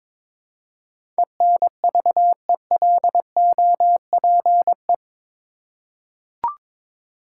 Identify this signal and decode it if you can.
Morse code, heard as a keyed tone, about 22 wpm, 700 Hz. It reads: ENVELOPE
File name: practice / 22wpm / Eight Letter Words - 22wpm